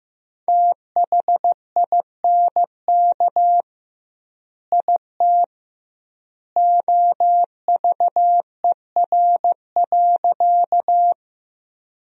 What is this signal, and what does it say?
Morse code, heard as a keyed tone, about 15 wpm, 700 Hz. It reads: THINK IT OVER.
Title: 2x Word-Spacing - Sentences from Top 200 Words - 15wpm